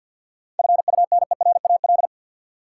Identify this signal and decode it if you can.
Morse code, heard as a keyed tone, about 52 wpm, 700 Hz. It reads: FUNERAL
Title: Seven Letter Words - 50wpm